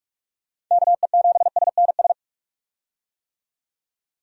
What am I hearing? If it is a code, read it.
Morse code, heard as a keyed tone, about 46 wpm, 700 Hz. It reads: KE7SNS
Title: USA Call Signs - 6 Characters - 45wpm